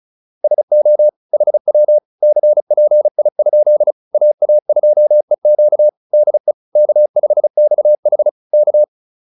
Morse code, heard as a keyed tone, about 35 wpm, 600 Hz. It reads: SO HW CPI? AA2EQ DE K5XH K